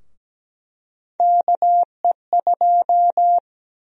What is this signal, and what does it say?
Morse code, heard as a keyed tone, about 17 wpm, 700 Hz. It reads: KE2
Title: USA Call Sign Prefixes - Repeated 2x - 17wpm